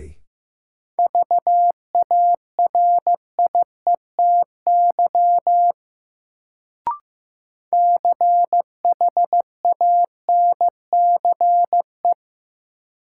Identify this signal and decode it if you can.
Morse code, heard as a keyed tone, about 16 wpm, 700 Hz. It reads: VARIETY CHANCE
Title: Sets of 3 Words - New Word Drill - 15wpm